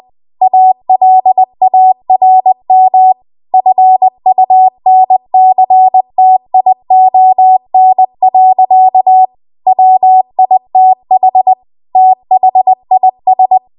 Morse code, heard as a keyed tone, about 20 wpm, 750 Hz. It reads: ALARM FUNCTION. WITH THIS
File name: Archive / 20 WPM / 20WPM